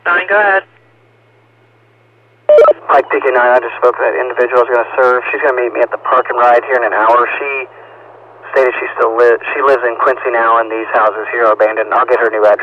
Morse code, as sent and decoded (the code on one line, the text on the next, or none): none